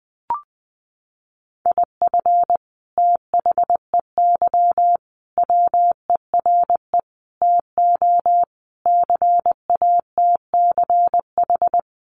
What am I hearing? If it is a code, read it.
Morse code, heard as a keyed tone, about 19 wpm, 700 Hz. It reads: IF THEY WERE TO CATCH
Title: Sentences from Top 1500 Words 20wpm